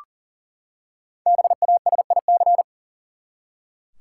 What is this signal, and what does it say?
Morse code, heard as a keyed tone, about 40 wpm, 700 Hz. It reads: BASIC
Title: Five Letter Words - 40wpm